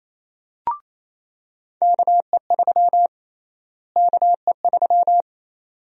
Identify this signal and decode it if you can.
Morse code, heard as a keyed tone, about 28 wpm, 700 Hz. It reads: KE3 KE3
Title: USA Call Sign Prefixes - Repeated 2x - 28wpm